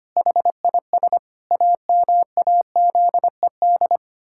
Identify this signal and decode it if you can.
Morse code, heard as a keyed tone, about 25 wpm, 700 Hz. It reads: HIS AMAZED
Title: Pangram Challenge v2 - 25wpm